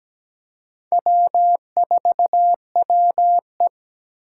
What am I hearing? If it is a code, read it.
Morse code, heard as a keyed tone, about 17 wpm, 700 Hz. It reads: W4WE
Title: Four Challenging Letter-Number Combinations 17wpm